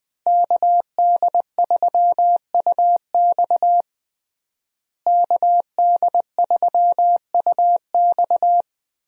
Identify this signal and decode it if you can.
Morse code, heard as a keyed tone, about 21 wpm, 700 Hz. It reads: KD3UX KD3UX
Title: USA Call Signs - 5 Characters - Repeated 3x - 20wpm